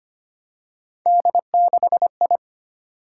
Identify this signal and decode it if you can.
Morse code, heard as a keyed tone, about 25 wpm, 700 Hz. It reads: D6I
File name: Three Letter-Number Combinations - 25wpm